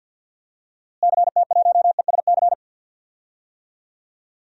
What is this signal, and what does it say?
Morse code, heard as a keyed tone, about 52 wpm, 700 Hz. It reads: CT1ESC
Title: DXCC-100 Call Signs - 6 Characters - Repeated 2x - 50wpm